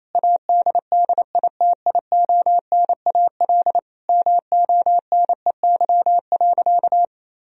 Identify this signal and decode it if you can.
Morse code, heard as a keyed tone, about 28 wpm, 700 Hz. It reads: ADDITIONAL MONEY.
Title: Sentences from Top 1000 Words - Review Part 4 - 28wpm